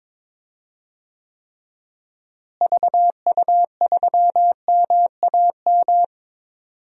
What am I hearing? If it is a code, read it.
Morse code, heard as a keyed tone, about 22 wpm, 700 Hz. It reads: VU3MAM
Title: DXCC-100 Call Signs - 6 Characters - Repeated 2x - 22wpm